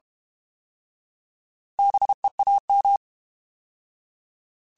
Morse code, heard as a keyed tone, about 32 wpm, 800 Hz. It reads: BEAM
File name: Alphabetical / Alphabetical mp3s / Bs @30